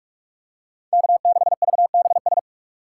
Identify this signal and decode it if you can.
Morse code, heard as a keyed tone, about 45 wpm, 700 Hz. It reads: K6VBS